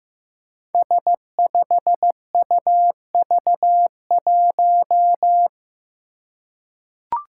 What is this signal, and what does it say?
Morse code, heard as a keyed tone, about 15 wpm, 700 Hz. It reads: S5UV1
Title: Five Challenging Letter-Number Combinations 15wpm